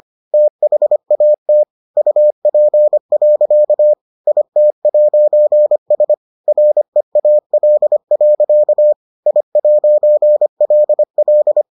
Morse code, heard as a keyed tone, about 25 wpm, 600 Hz. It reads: THAT UP. IT'S REAL. I'LL